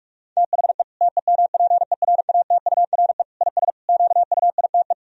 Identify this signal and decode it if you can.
Morse code, heard as a keyed tone, about 46 wpm, 700 Hz. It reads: THE TEMPERATURE IS QUITE
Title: Sentences from Top 1000 Words - Review Part 4 - 45wpm